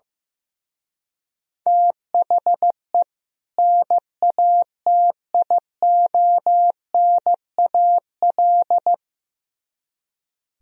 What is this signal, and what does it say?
Morse code, heard as a keyed tone, about 15 wpm, 700 Hz. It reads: THE NATIONAL
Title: Speed-Racing - Sets of 2 Words 15wpm